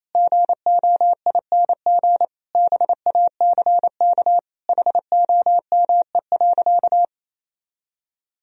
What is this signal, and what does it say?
Morse code, heard as a keyed tone, about 28 wpm, 700 Hz. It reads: GOING BACK HOME.
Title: Sentences from Top 500 Words and Abbreviations 28wpm